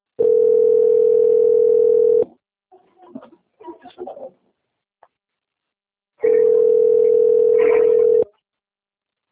Ringback tone (North America), 440 + 480 Hz.